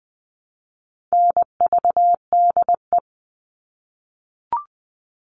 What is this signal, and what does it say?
Morse code, heard as a keyed tone, about 20 wpm, 700 Hz. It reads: NVDE
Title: All Cut Numbers - 4-Digit 20wpm